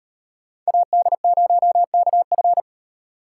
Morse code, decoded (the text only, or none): AD0KF